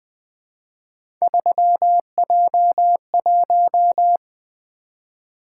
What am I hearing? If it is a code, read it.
Morse code, heard as a keyed tone, about 20 wpm, 700 Hz. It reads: 3J1
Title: Three Letter-Number Combinations - 20wpm